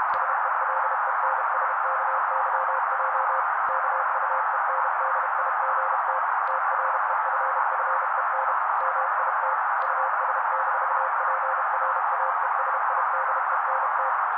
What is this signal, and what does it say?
Morse code, heard as a keyed tone, about 32 wpm, 530 Hz. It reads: SELENSKYJ KUENDIGT TREFFEN MIT US-PRAESIDENT
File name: daily / 31wpm noise